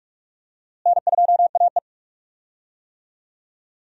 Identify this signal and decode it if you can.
Morse code, heard as a keyed tone, about 46 wpm, 700 Hz. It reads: N2AE
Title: USA Call Signs - 4 Characters - Repeated 3x - 45wpm